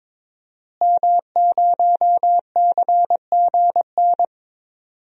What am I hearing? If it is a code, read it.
Morse code, heard as a keyed tone, about 22 wpm, 700 Hz. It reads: M0CGN
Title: DXCC-100 Call Signs - 5 Characters - Repeated 3x - 22wpm